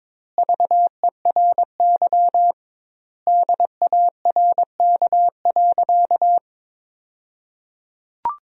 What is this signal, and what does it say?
Morse code, heard as a keyed tone, about 22 wpm, 700 Hz. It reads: VERY DARK.
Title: 2x Word-Spacing - Sentences from Top 500 Words - 22wpm